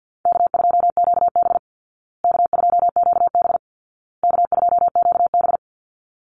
Morse code, heard as a keyed tone, about 50 wpm, 700 Hz. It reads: K2QB K2QB K2QB
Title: Easy Call Signs Repeated 3x - 50wpm